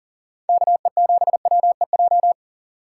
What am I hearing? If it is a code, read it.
Morse code, heard as a keyed tone, about 40 wpm, 700 Hz. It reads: KE7WEJ
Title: USA Call Signs - 6 Characters - 3x Character Spacing - 40wpm